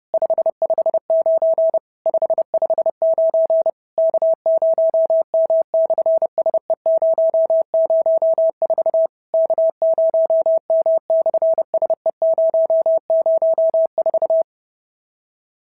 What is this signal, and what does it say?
Morse code, heard as a keyed tone, about 30 wpm, 650 Hz. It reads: 559 559 K0M/SE004 K0M/SE004
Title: SOTA QSO Practice 30wpm